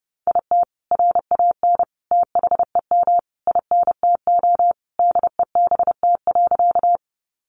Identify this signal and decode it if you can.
Morse code, heard as a keyed tone, about 30 wpm, 700 Hz. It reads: IT RAN THEM INTO DEBT.
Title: Sentences from Top 1300 Words 30wpm